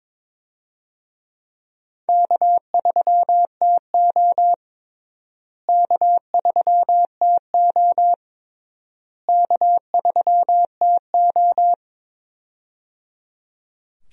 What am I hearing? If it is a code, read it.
Morse code, heard as a keyed tone, about 22 wpm, 700 Hz. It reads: K3TO K3TO K3TO
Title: USA Call Signs - 4 Characters - Repeated 3x - 22wpm